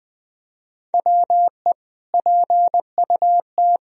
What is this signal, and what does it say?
Morse code, heard as a keyed tone, about 21 wpm, 700 Hz. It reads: WE PUT
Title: Sentences from Top 1000 Words - Review Part 4 - 20wpm